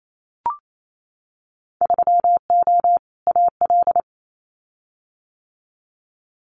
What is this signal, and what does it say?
Morse code, heard as a keyed tone, about 28 wpm, 700 Hz. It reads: 3O AL
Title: Winter Field Day Exchanges Encore 28wpm